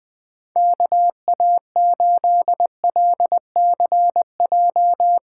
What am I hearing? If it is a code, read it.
Morse code, heard as a keyed tone, about 20 wpm, 700 Hz. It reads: KA8LCJ
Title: USA Callsign DE Callsign - 6 Characters - 20wpm